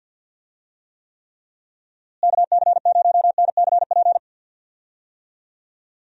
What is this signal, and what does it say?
Morse code, heard as a keyed tone, about 49 wpm, 700 Hz. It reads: KC0NCP
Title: USA Call Signs - 6 Characters - Repeated 3x - 50wpm